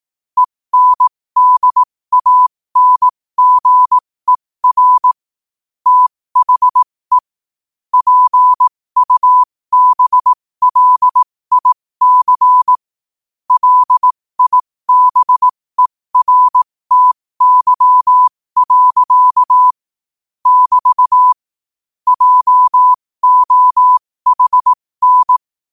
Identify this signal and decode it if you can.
Morse code, heard as a keyed tone, about 18 wpm, spaced as for 15 wpm, 1000 Hz. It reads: ENDANGER THE PUBLIC LIBERTY. <BT> JOHN
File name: qotdfiles / 15wpm